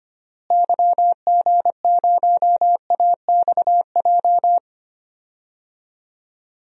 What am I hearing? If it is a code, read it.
Morse code, heard as a keyed tone, about 25 wpm, 690 Hz. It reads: YG0AXJ
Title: DXCC-100 Call Signs - 6 Characters - Repeated 3x - 25wpm